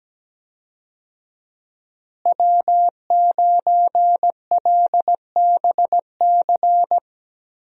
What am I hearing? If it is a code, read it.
Morse code, heard as a keyed tone, about 17 wpm, 700 Hz. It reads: W9LBC
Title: USA Call Signs - 5 Characters - Repeated 3x - 17wpm